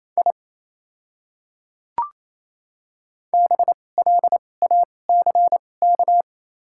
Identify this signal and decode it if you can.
Morse code, heard as a keyed tone, about 28 wpm, spaced as for 20 wpm, 700 Hz. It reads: I BLACK
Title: 2x Character Spacing - Top 500 Words - 28wpm